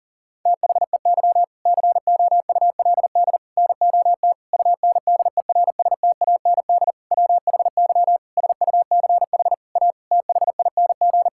Morse code, heard as a keyed tone, about 40 wpm, 700 Hz. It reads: THEY COULD NOT UNDERSTAND WHY SUCH A THING
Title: Sentences from Top 1000 Words - Review Part 6 - 40wpm